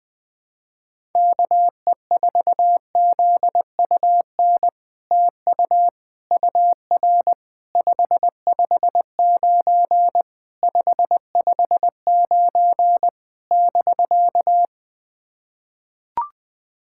Morse code, heard as a keyed tone, about 21 wpm, 700 Hz. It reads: KE4ZUN TU UR 559 559 <BK>
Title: POTA QSO Practice - 20wpm